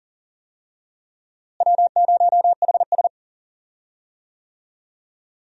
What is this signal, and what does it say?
Morse code, heard as a keyed tone, about 38 wpm, 700 Hz. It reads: W0HS
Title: USA Call Signs - 4 Characters - Repeated 2x - 40wpm